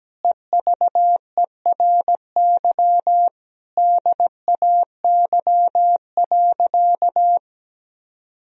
Morse code, heard as a keyed tone, about 17 wpm, 700 Hz. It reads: EVERY DAY.
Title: Sentences from Top 1000 Words - Review Part 1 - 17wpm